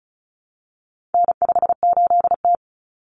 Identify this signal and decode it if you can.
Morse code, heard as a keyed tone, about 34 wpm, 700 Hz. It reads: N58T
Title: Common Cut Numbers - 4-Digit 35wpm